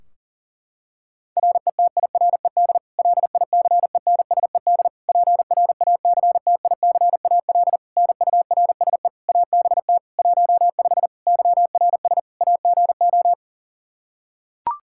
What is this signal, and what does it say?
Morse code, heard as a keyed tone, about 40 wpm, 700 Hz. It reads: RETIRED LICENSED PRACTICAL NURSE ABT 15 YRS AGO
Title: K1USN SST QSO Practice 40wpm